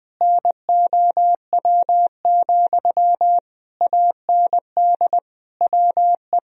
Morse code, heard as a keyed tone, about 21 wpm, 700 Hz. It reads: NOW, AND WE
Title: Sentences from Top 1000 Words - Review Part 4 - 20wpm